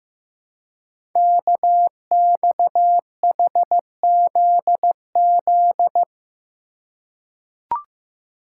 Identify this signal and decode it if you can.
Morse code, heard as a keyed tone, about 15 wpm, 700 Hz. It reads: KXHZZ
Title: Code Groups - 5 Letters - 15wpm